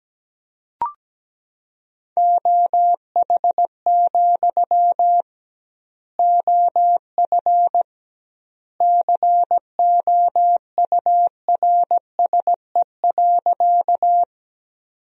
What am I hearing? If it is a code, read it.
Morse code, heard as a keyed tone, about 17 wpm, 700 Hz. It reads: OH, OF COURSE.
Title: 2x Word-Spacing - Sentences from Top 500 Words - 17wpm